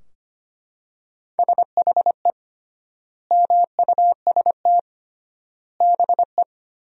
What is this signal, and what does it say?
Morse code, heard as a keyed tone, about 25 wpm, 700 Hz. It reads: SHE MUST BE
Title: 3x Word-Spacing - Sentences from Top 400 Words - 25wpm